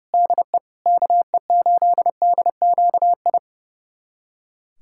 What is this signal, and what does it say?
Morse code, heard as a keyed tone, about 30 wpm, 700 Hz. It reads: DE KE8DQI
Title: USA Callsign DE Callsign 30wpm